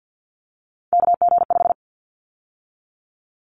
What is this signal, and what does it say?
Morse code, heard as a keyed tone, about 50 wpm, 700 Hz. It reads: KG5
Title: Callsign Prefixes for Top 25 DXCC - 50wpm